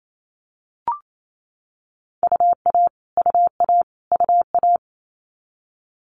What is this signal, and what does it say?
Morse code, heard as a keyed tone, about 28 wpm, 700 Hz. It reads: UA UA UA
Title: Multi-letter Phonemes Repeated 3x 28wpm